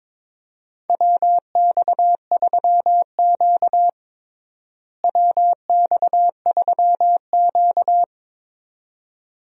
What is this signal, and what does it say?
Morse code, heard as a keyed tone, about 22 wpm, 700 Hz. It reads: WX3Q WX3Q
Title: USA Call Signs - 4 Characters - Repeated 2x - 22wpm